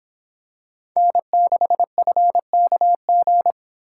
Morse code, heard as a keyed tone, about 25 wpm, 700 Hz. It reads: N6FKG